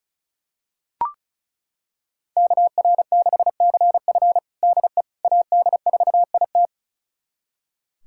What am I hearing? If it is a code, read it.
Morse code, heard as a keyed tone, about 36 wpm, 690 Hz. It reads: KR6CF DE AD4IT
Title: USA Callsign DE Callsign - 5 Characters - 35wpm